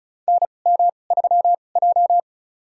Morse code, heard as a keyed tone, about 36 wpm, spaced as for 25 wpm, 700 Hz. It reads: NM3J
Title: USA Call Signs - 2x Character Spacing - 35wpm